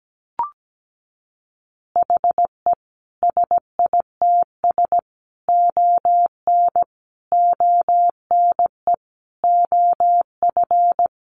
Morse code, heard as a keyed tone, about 17 wpm, 700 Hz. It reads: HE SITS ON ONE OF
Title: Additional Easy Phrases 17wpm